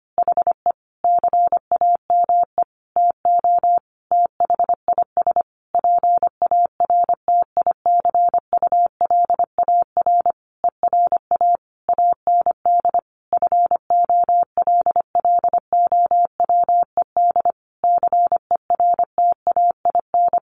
Morse code, heard as a keyed tone, about 25 wpm, 700 Hz. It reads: HE CAME TO THIS PARTICULAR ERA AND FOLLOWED CERTAIN
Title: A Sound Of Thunder 25wpm